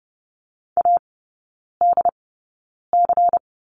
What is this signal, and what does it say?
Morse code, heard as a keyed tone, about 30 wpm, spaced as for 9 wpm, 690 Hz. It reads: ADC
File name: Lesson 16e - Letter c - Three Characters 30wpm